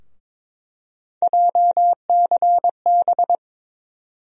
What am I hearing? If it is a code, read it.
Morse code, heard as a keyed tone, about 21 wpm, 700 Hz. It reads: JCB
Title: USA Call Sign Postfixes - 3 Letters 22wpm